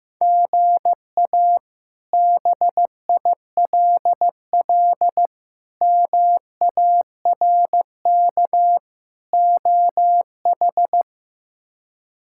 Morse code, heard as a keyed tone, about 15 wpm, 700 Hz. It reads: GA BILL MARK OH
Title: K1USN SST QSO Practice - 15wpm